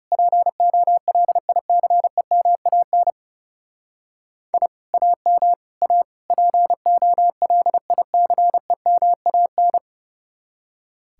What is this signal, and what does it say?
Morse code, heard as a keyed tone, about 35 wpm, 700 Hz. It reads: POLICEMAN I AM A POLICEMAN
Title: Speed-Racing Common QSO Phrases - 30wpm